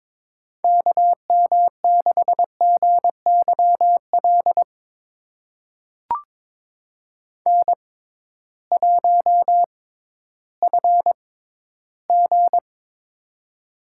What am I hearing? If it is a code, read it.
Morse code, heard as a keyed tone, about 22 wpm, spaced as for 13 wpm, 700 Hz. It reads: KM6GYL N1FG
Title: USA Call Signs - 6x Character Spacing - 22wpm